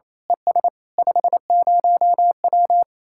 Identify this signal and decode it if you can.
Morse code, heard as a keyed tone, about 27 wpm, 700 Hz. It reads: ES 50W
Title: Speed-Racing Common QSO Phrases Encore - 28wpm